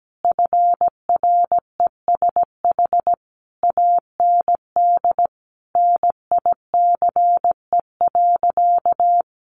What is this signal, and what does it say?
Morse code, heard as a keyed tone, about 17 wpm, 700 Hz. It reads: FRESH AND NICE.
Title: Sentences from Top 1200 Words 17wpm